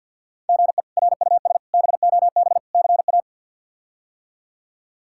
Morse code, heard as a keyed tone, about 53 wpm, 700 Hz. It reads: GE RUS BOB CA
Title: K1USN SST QSO Practice - 50wpm